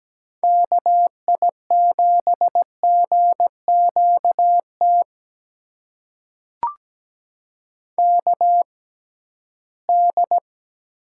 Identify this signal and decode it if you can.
Morse code, heard as a keyed tone, about 17 wpm, spaced as for 13 wpm, 700 Hz. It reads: KI7GQT KD
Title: USA Call Signs - 6 Characters - 6x Character Spacing - 17wpm